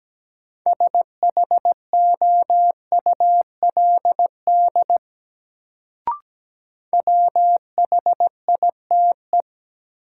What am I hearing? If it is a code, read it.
Morse code, heard as a keyed tone, about 17 wpm, 700 Hz. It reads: SHOULD WHITE
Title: Sets of 2 Words - New Word Drill - 17wpm